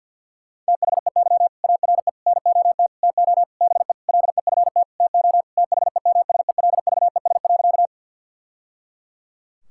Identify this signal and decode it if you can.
Morse code, heard as a keyed tone, about 50 wpm, 700 Hz. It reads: THEY ARE NOT TO BE LEFT TO THEMSELVES.
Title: Sentences from Top 1000 Words - Review Part 5 - 50wpm